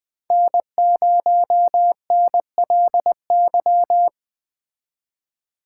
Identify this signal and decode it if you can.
Morse code, heard as a keyed tone, about 20 wpm, 700 Hz. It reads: N0NLY
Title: Speed Racing - USA 1x3 Call Signs 17wpm